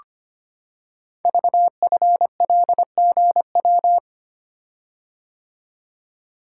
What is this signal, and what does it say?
Morse code, heard as a keyed tone, about 25 wpm, 700 Hz. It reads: VFLGW
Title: Code Groups - 5 Letters - 25wpm